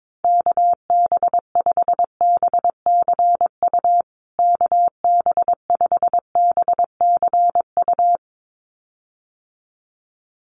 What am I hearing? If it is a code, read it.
Morse code, heard as a keyed tone, about 22 wpm, 700 Hz. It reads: KB5BCU KB5BCU
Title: USA Call Signs Repeated 2x 22wpm